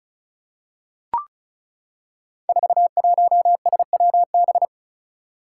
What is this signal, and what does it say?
Morse code, heard as a keyed tone, about 36 wpm, 700 Hz. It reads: 41SWB
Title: Five Challenging Letter-Number Combinations 35wpm